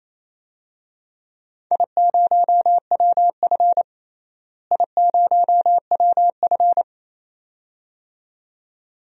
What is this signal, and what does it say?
Morse code, heard as a keyed tone, about 28 wpm, 700 Hz. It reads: I0WF I0WF
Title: DXCC-100 Call Signs - 4 Characters - Repeated 2x - 28wpm